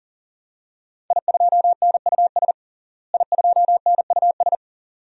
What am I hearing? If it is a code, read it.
Morse code, heard as a keyed tone, about 40 wpm, 700 Hz. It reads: I2NUS I2NUS
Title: DXCC-100 Call Signs - 5 Characters - Repeated 2x - 40wpm